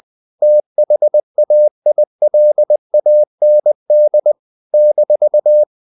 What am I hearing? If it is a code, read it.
Morse code, heard as a keyed tone, about 20 wpm, 600 Hz. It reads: THAILAND -
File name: CW / Practice / NationalCapitals20